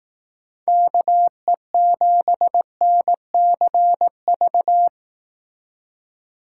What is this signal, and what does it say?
Morse code, heard as a keyed tone, about 18 wpm, 700 Hz. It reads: KE7NCV
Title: Speed Racing - USA 2x3 Call Signs 15wpm